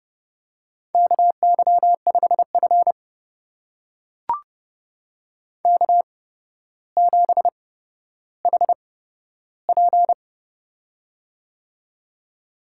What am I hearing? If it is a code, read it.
Morse code, heard as a keyed tone, about 29 wpm, spaced as for 13 wpm, 700 Hz. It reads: KY5F K7HP
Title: USA Call Signs - 4 Characters - 8x Character Spacing - 30wpm